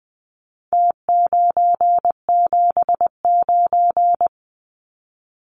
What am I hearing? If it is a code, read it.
Morse code, heard as a keyed tone, about 20 wpm, 700 Hz. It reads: T979
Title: Numbers - Contest Style - 4-Digit 20wpm